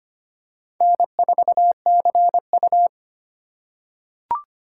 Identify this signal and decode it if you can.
Morse code, heard as a keyed tone, about 25 wpm, 700 Hz. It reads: N4CU
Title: USA Call Signs - 4 Characters - 2x Character Spacing - 25wpm